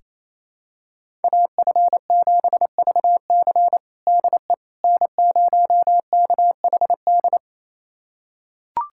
Morse code, heard as a keyed tone, about 28 wpm, 700 Hz. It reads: AF7VC DE N0KHD
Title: USA Callsign DE Callsign - 5 Characters - 28wpm